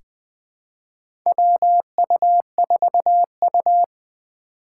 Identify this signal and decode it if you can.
Morse code, heard as a keyed tone, about 21 wpm, 700 Hz. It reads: WU4U
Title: USA Call Signs - 4 Characters - 3x Character Spacing - 20wpm